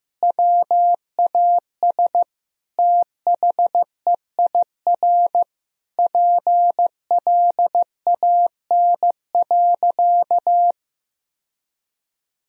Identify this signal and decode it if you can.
Morse code, heard as a keyed tone, about 15 wpm, 700 Hz. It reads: WAS THEIR PLAN.
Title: Sentences from Top 1000 Words - Review Part 2 - 15wpm